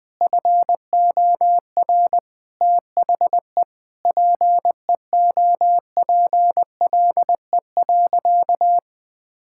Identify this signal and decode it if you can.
Morse code, heard as a keyed tone, about 20 wpm, 700 Hz. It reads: FOR THE PEOPLE.
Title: Sentences from Top 1000 Words - Review Part 5 - 20wpm